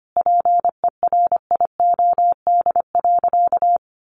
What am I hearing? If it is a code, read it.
Morse code, heard as a keyed tone, about 25 wpm, 700 Hz. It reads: PERIOD.
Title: Sentences from Top 800 Words 25wpm